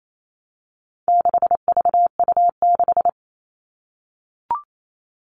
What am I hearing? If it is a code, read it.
Morse code, heard as a keyed tone, about 28 wpm, 700 Hz. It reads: 6VU6
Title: All Cut Numbers - 4-Digit 28wpm